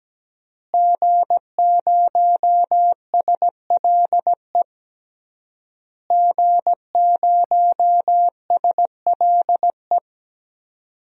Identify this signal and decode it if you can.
Morse code, heard as a keyed tone, about 17 wpm, 700 Hz. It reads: G0SLE G0SLE
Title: DXCC-100 Call Signs - 5 Characters - Repeated 3x - 17wpm